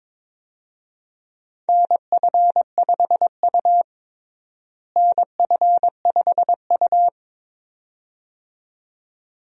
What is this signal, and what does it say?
Morse code, heard as a keyed tone, about 22 wpm, 700 Hz. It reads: NF5U NF5U
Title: USA Call Signs - 4 Characters - Repeated 2x - 22wpm